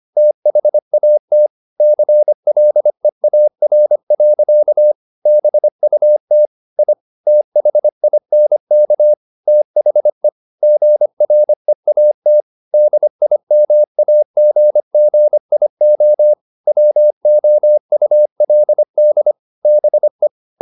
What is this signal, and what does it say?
Morse code, heard as a keyed tone, about 25 wpm, 600 Hz. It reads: THAT CLEAR. BUT I THINK THE GREAT DIMAGGIO WOULD BE